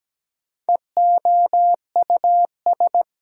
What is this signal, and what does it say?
Morse code, heard as a keyed tone, about 17 wpm, 700 Hz. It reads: EOUS
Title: Prefixes and Suffixes 17wpm